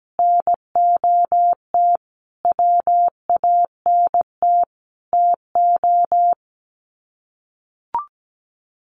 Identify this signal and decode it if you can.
Morse code, heard as a keyed tone, about 17 wpm, 700 Hz. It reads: NOT WANT TO
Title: Sets of 5 Words Encore 17wpm